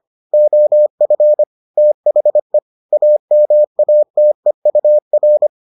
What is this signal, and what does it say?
Morse code, heard as a keyed tone, about 25 wpm, 600 Hz. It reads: OF THE AMATEUR